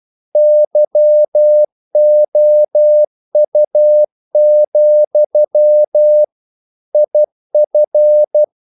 Morse code, heard as a keyed tone, about 12 wpm, 600 Hz. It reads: YOU, IF
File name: CW / Practice / If12